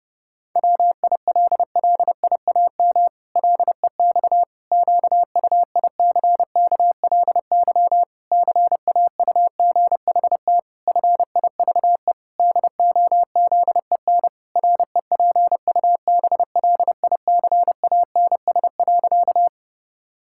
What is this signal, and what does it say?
Morse code, heard as a keyed tone, about 32 wpm, 700 Hz. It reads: WILLIAM LEX QUICKLY CAUGHT FIVE DOZEN REPUBLICANS.
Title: Pangram Challenge v2 - 30wpm